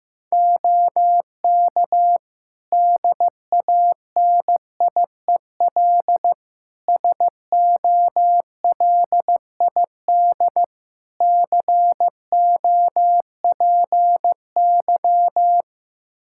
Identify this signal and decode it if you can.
Morse code, heard as a keyed tone, about 15 wpm, 700 Hz. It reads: OK DANIEL SOLID COPY
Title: K1USN SST QSO Practice 15wpm